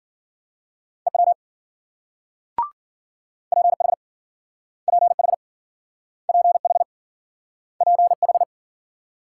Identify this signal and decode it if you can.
Morse code, heard as a keyed tone, about 53 wpm, 700 Hz. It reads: EW PH PH PH PH